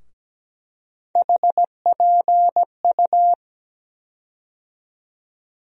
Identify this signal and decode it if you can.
Morse code, heard as a keyed tone, about 17 wpm, 700 Hz. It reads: HPU
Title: USA Call Sign Postfixes - 3 Letters 17wpm